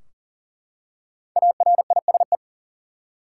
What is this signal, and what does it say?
Morse code, heard as a keyed tone, about 40 wpm, 700 Hz. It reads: ARISE
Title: Five Letter Words - 40wpm